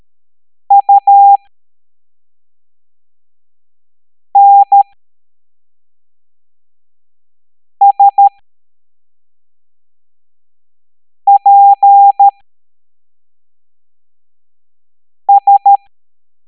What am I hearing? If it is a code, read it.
Morse code, heard as a keyed tone, about 13 wpm, spaced as for 3 wpm, 800 Hz. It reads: UNSPS